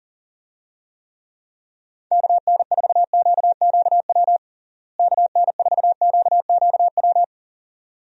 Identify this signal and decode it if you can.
Morse code, heard as a keyed tone, about 40 wpm, 700 Hz. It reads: KN4QQW KN4QQW
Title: USA Call Signs - 6 Characters - Repeated 2x - 40wpm